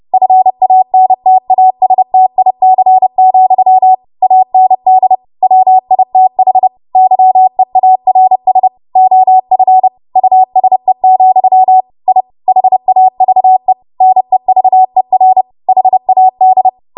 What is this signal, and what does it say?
Morse code, heard as a keyed tone, about 30 wpm, 750 Hz. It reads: FANTASTIC, AND WITH YEARS OF USE, I HAVE NEVER HAD
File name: Morse / 30WPM